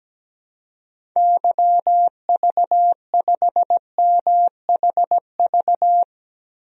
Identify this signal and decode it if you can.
Morse code, heard as a keyed tone, about 17 wpm, 700 Hz. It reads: YV5MHV
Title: DXCC-100 Call Signs - 6 Characters - Repeated 3x - 17wpm